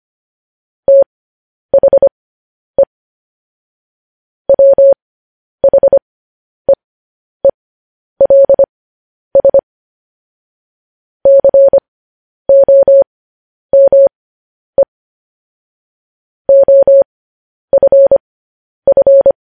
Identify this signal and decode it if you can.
Morse code, heard as a keyed tone, about 25 wpm, spaced as for 10 wpm, 550 Hz. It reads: THE WHEELS COME OFF